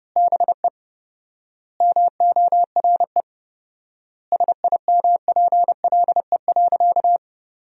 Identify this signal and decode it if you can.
Morse code, heard as a keyed tone, about 30 wpm, 700 Hz. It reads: BE MORE SIMPLE.
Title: 4x Word-Spacing - Sentences from Top 500 Words - 30wpm